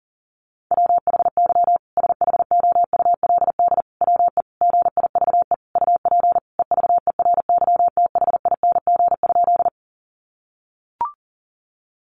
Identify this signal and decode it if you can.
Morse code, heard as a keyed tone, about 40 wpm, 700 Hz. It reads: WHY SHOULD WE GIVE UP EVERYTHING?
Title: Sentences from Top 700 Words 40wpm